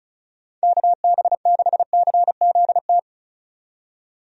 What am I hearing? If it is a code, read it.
Morse code, heard as a keyed tone, about 36 wpm, 700 Hz. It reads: KB6CZT
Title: USA Call Signs - 6 Characters - 35wpm